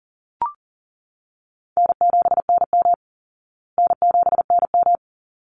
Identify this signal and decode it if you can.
Morse code, heard as a keyed tone, about 40 wpm, 700 Hz. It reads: N7NM N7NM
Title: Easy Call Signs Repeated 2x - 40wpm